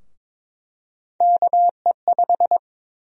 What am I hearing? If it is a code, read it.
Morse code, heard as a keyed tone, about 22 wpm, 700 Hz. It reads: KE5
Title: USA Call Sign Prefixes - Repeated 2x - 22wpm